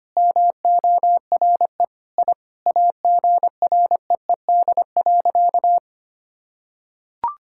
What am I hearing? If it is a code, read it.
Morse code, heard as a keyed tone, about 25 wpm, 700 Hz. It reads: MORE I AGREED.
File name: practice / 25wpm / Sentences with 9-Word Phrases - 25wpm